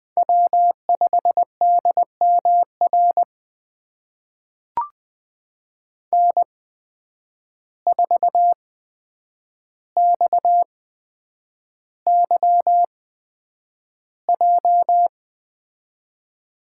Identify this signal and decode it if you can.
Morse code, heard as a keyed tone, about 20 wpm, spaced as for 9 wpm, 700 Hz. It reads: W5DMR N4XYJ
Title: USA Call Signs - 5 Characters - 8x Character Spacing - 20wpm